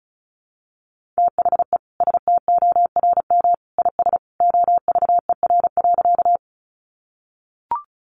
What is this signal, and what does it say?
Morse code, heard as a keyed tone, about 34 wpm, 700 Hz. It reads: THE STORM IS OVER.
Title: Sentences from Top 1500 Words 35wpm